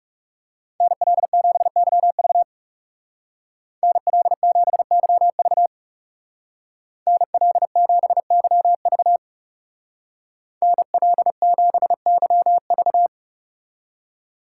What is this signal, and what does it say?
Morse code, heard as a keyed tone, about 36 wpm, 700 Hz. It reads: NL7YV NL7YV NL7YV NL7YV
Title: Speed Racing - USA 2x2 Call Signs 30wpm